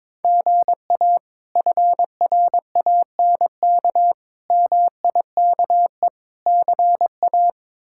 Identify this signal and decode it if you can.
Morse code, heard as a keyed tone, about 22 wpm, 700 Hz. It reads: GA FRANK MIKE CA
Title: K1USN SST QSO Practice - 22wpm